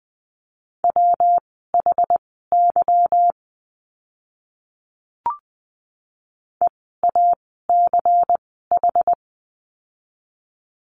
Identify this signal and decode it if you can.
Morse code, heard as a keyed tone, about 20 wpm, spaced as for 15 wpm, 700 Hz. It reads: WHY EACH
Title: 192 Lesson 31f - Letter v - Words 20wpm